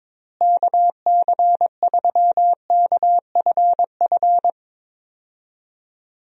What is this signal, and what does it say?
Morse code, heard as a keyed tone, about 22 wpm, 700 Hz. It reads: KC3KFF
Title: USA Call Signs - 6 Characters - Repeated 2x - 22wpm